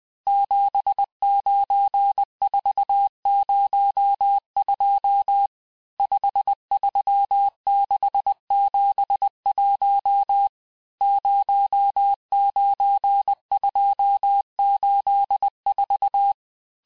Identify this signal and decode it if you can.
Morse code, heard as a keyed tone, about 20 wpm, 800 Hz. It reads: 79402 53671 09284
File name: cours-1584-teste-chiffre-vitesse-19mpm